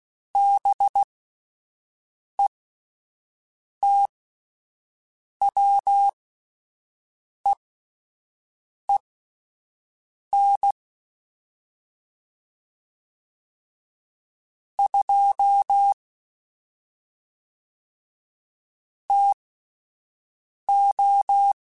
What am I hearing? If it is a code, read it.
Morse code, heard as a keyed tone, about 16 wpm, spaced as for 5 wpm, 800 Hz. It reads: BETWEEN 2 TO
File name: Spaced Characters / Spaced Characters-mp3s / Weather 4 14 sp @10